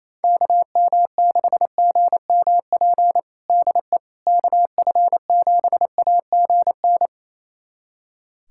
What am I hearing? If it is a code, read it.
Morse code, heard as a keyed tone, about 28 wpm, 700 Hz. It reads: KM6GMP DE KF7AGN
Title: USA Callsign DE Callsign - 6 Characters - 28wpm